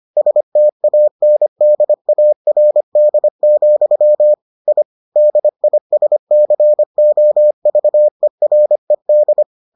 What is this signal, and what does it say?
Morse code, heard as a keyed tone, about 25 wpm, 600 Hz. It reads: STANDARD, I DISCOVERED